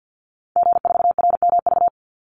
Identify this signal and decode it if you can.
Morse code, heard as a keyed tone, about 48 wpm, 700 Hz. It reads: G4RMV
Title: Call Signs Repeated 3x - 50wpm